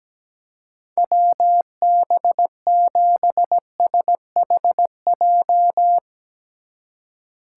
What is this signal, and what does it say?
Morse code, heard as a keyed tone, about 17 wpm, 700 Hz. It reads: WB7SHJ
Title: USA Call Signs - 6 Characters - 3x Character Spacing - 17wpm